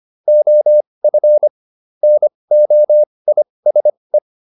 Morse code, heard as a keyed tone, about 25 wpm, spaced as for 20 wpm, 600 Hz. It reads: OF NOISE